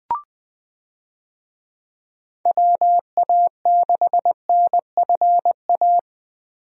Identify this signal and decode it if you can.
Morse code, heard as a keyed tone, about 20 wpm, 700 Hz. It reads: WA6NFA
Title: USA Call Signs - 6 Characters - Repeated 2x - 20wpm